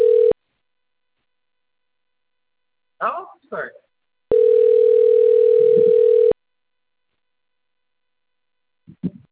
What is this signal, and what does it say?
Ringback tone (North America), 440 + 480 Hz, with speech.
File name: rg-1600-unknown- - - .79707